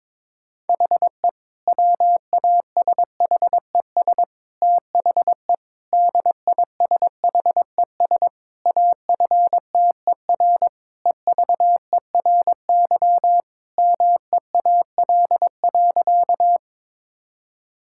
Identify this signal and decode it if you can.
Morse code, heard as a keyed tone, about 22 wpm, 700 Hz. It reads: HE WASHES THE DISHES AFTER EVERY MEAL.
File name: Difficult Sentences with Hints - 22wpm